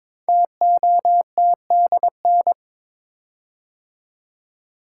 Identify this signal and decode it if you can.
Morse code, heard as a keyed tone, about 22 wpm, 700 Hz. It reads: TOTDN
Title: Code Groups - 5 Letters - 22wpm